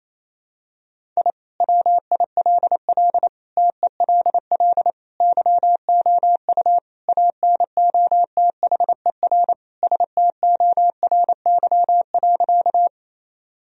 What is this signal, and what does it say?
Morse code, heard as a keyed tone, about 28 wpm, 700 Hz. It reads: I WILL TELL YOU ANOTHER STORY.
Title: Sentences from Top 1000 Words - Review Part 1 - 28wpm